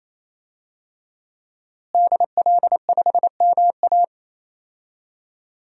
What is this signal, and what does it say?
Morse code, heard as a keyed tone, about 28 wpm, 700 Hz. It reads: DL5MA
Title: DXCC-100 Call Signs - 5 Characters - Repeated 3x - 28wpm